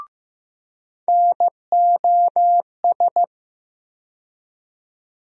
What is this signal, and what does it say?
Morse code, heard as a keyed tone, about 15 wpm, 700 Hz. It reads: NOS
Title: USA Call Sign Postfixes - 3 Letters 15wpm